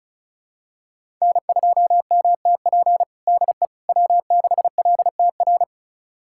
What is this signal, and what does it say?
Morse code, heard as a keyed tone, about 34 wpm, 700 Hz. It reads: N2MTP DE W6LTR
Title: USA Callsign DE Callsign - 5 Characters - 35wpm